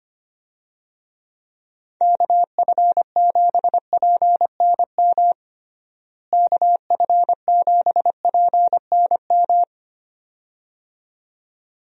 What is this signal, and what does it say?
Morse code, heard as a keyed tone, about 25 wpm, 700 Hz. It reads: KF7PNM KF7PNM
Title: USA Call Signs - 6 Characters - Repeated 2x - 25wpm